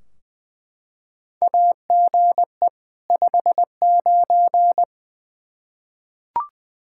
Morse code, heard as a keyed tone, about 20 wpm, 700 Hz. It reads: AGE 59
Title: Practice QSO - Exchange 3 - More Info - 20wpm